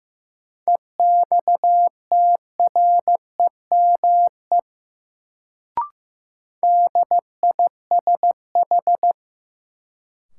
Morse code, heard as a keyed tone, about 15 wpm, 700 Hz. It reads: EXTREME DISH
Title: Sentences from Top 1300 Words - New Word Drill - 15wpm